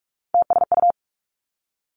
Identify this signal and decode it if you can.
Morse code, heard as a keyed tone, about 46 wpm, 700 Hz. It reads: TSU
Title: Three Challenging Letter-Number Combinations 45wpm